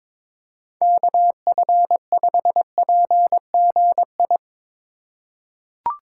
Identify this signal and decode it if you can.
Morse code, heard as a keyed tone, about 22 wpm, 700 Hz. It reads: KF5PGI
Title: USA Call Signs - 6 Characters - 4x Character Spacing - 22wpm